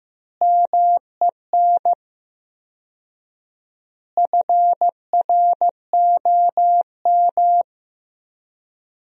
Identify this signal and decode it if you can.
Morse code, heard as a keyed tone, about 15 wpm, 700 Hz. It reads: MEN FROM
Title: 4x Word-Spacing - Sentences from Top 300 Words - 15wpm